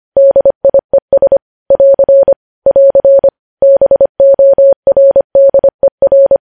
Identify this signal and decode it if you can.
Morse code, heard as a keyed tone, about 25 wpm, 550 Hz. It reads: DIES <AR> <AR> BORDER
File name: podcast / News headlines-1x-25x25